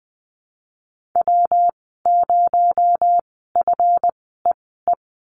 Morse code, heard as a keyed tone, about 20 wpm, spaced as for 15 wpm, 700 Hz. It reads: W0FEE
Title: 219 Lesson 35g - Number 0 - Call Signs 20wpm